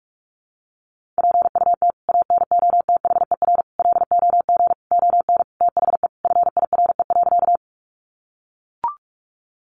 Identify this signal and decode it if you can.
Morse code, heard as a keyed tone, about 44 wpm, 700 Hz. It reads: PUT ANOTHER LOG ON THE FIRE.
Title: Sentences from Top 1400 Words 45wpm